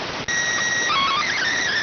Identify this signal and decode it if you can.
SSTV (slow-scan television), PD 290.